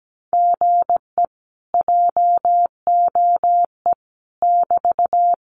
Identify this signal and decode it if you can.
Morse code, heard as a keyed tone, about 17 wpm, 700 Hz. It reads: GE JOE <BT>
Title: Common QSO Phrases Encore 17wpm